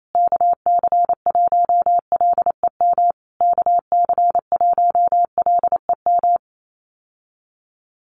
Morse code, heard as a keyed tone, about 28 wpm, 700 Hz. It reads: KC1LEM KC1LEM
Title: USA Call Signs Repeated 3x 28wpm